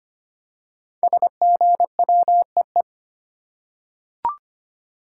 Morse code, heard as a keyed tone, about 25 wpm, 700 Hz. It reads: SGWEE
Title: Code Groups - 5 Letters - 25wpm